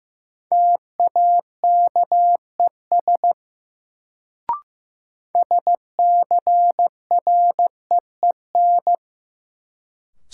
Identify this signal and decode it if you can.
Morse code, heard as a keyed tone, about 15 wpm, 700 Hz. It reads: TAKES SCREEN
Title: Sentences from Top 1000 Words - Review Part 6 - New Word Drill - 15wpm